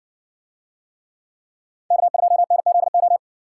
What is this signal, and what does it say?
Morse code, heard as a keyed tone, about 60 wpm, 700 Hz. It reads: K2NZO